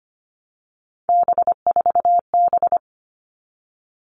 Morse code, heard as a keyed tone, about 25 wpm, 700 Hz. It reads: B4B
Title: All Cut Numbers - 3-Digit 25wpm